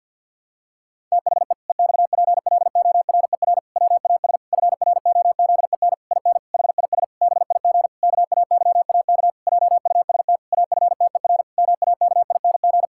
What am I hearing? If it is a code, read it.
Morse code, heard as a keyed tone, about 49 wpm, 700 Hz. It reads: THE EXPLORER WAS FROZEN IN HIS BIG KAYAK JUST AFTER MAKING
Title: Pangram Challenge v2 - 50wpm